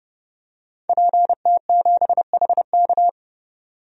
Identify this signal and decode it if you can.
Morse code, heard as a keyed tone, about 32 wpm, 700 Hz. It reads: PT7HK
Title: DXCC-100 Call Signs - 5 Characters - 30wpm